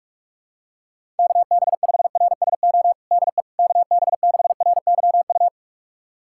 Morse code, heard as a keyed tone, about 46 wpm, 700 Hz. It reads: KB5RSO DE KB6RYU
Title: USA Callsign DE Callsign - 6 Characters - 45wpm